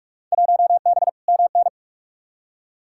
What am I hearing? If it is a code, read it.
Morse code, heard as a keyed tone, about 42 wpm, 700 Hz. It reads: 1B MN
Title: Field Day Exchanges Encore - 45wpm